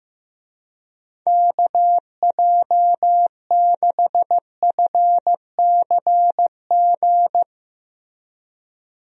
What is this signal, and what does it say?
Morse code, heard as a keyed tone, about 15 wpm, 700 Hz. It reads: KJ6FCG
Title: USA Call Signs - 6 Characters - Repeated 3x - 15wpm